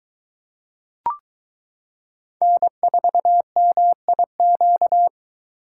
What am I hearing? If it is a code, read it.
Morse code, heard as a keyed tone, about 22 wpm, 700 Hz. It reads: N4MIQ